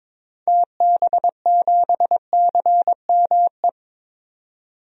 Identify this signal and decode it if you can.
Morse code, heard as a keyed tone, about 22 wpm, 700 Hz. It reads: TB7CME
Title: DXCC-100 Call Signs - 6 Characters - 22wpm